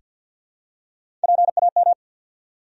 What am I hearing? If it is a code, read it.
Morse code, heard as a keyed tone, about 49 wpm, 700 Hz. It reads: PAM
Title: USA Common Names - Max 4 Letters - 50wpm